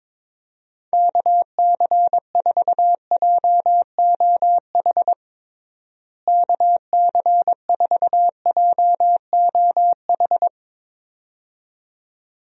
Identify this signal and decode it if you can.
Morse code, heard as a keyed tone, about 22 wpm, 700 Hz. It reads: KC4JOH KC4JOH
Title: USA Call Signs - 6 Characters - Repeated 2x - 22wpm